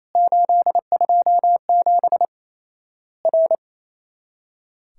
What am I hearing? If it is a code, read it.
Morse code, heard as a keyed tone, about 28 wpm, 700 Hz. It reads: 827 R
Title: Worked All Europe DX Contest - QTC Exchange - 28wpm